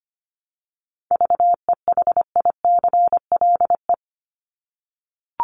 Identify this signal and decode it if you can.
Morse code, heard as a keyed tone, about 25 wpm, 700 Hz. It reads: VEHICLE
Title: Top 1751-2000 Words 25wpm